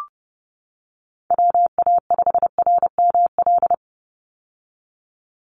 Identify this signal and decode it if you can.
Morse code, heard as a keyed tone, about 30 wpm, 700 Hz. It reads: WA5RML
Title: Call Signs - 30wpm